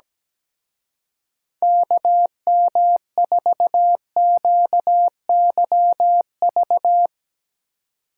Morse code, heard as a keyed tone, about 17 wpm, 700 Hz. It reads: KM4QYV
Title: Speed Racing - USA 2x3 Call Signs 17wpm